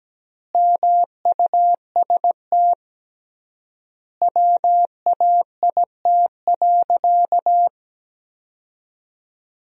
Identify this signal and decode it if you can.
Morse code, heard as a keyed tone, about 17 wpm, 700 Hz. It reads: MUST WAIT.
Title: 3x Word-Spacing - Sentences from Top 500 Words - 17wpm